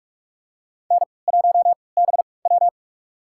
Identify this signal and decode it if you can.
Morse code, heard as a keyed tone, about 45 wpm, spaced as for 25 wpm, 700 Hz. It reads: N1BW
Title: USA Call Signs - 4 Characters - 3x Character Spacing - 45wpm